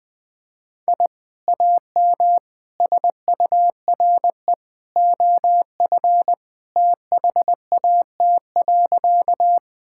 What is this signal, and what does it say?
Morse code, heard as a keyed tone, about 21 wpm, 700 Hz. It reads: I AM SURE OF THAT.
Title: Sentences with 2-Word Phrases - 20wpm